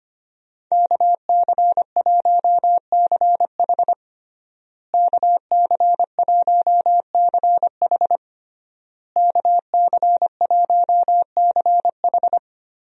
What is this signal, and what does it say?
Morse code, heard as a keyed tone, about 25 wpm, 700 Hz. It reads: KC1CH KC1CH KC1CH
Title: USA Call Signs - 5 Characters - Repeated 3x - 25wpm